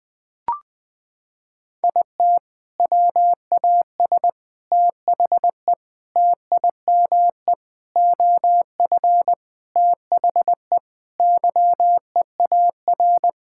Morse code, heard as a keyed tone, about 20 wpm, 700 Hz. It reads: IT WAS THE TIME OF THE YEAR
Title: Sentences with 4-Word Phrases - 20wpm